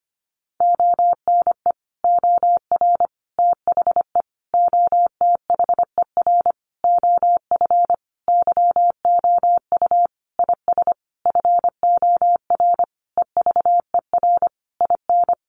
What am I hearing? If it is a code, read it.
Morse code, heard as a keyed tone, about 25 wpm, 700 Hz. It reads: ONE OR THE OTHER OF YOU IS FOR EVER IN
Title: Sentences from Top 400 Words 25wpm